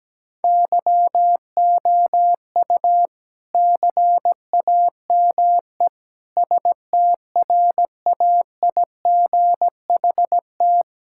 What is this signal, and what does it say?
Morse code, heard as a keyed tone, about 17 wpm, 700 Hz. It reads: YOU CAME STRAIGHT
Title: Sentences from Top 1000 Words - Review Part 6 - 17wpm